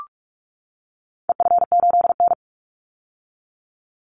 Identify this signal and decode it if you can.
Morse code, heard as a keyed tone, about 44 wpm, 700 Hz. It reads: EF8N
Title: Call Signs - 45wpm